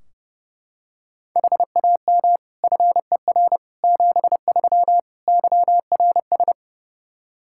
Morse code, heard as a keyed tone, about 30 wpm, 700 Hz. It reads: HAM FER 73 YRS
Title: Practice QSO - Exchange 3 - More Info - 30wpm